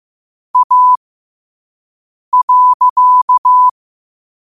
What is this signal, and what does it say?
Morse code, heard as a keyed tone, about 15 wpm, spaced as for 5 wpm, 1000 Hz. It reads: A.
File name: qotdfiles / 5wpm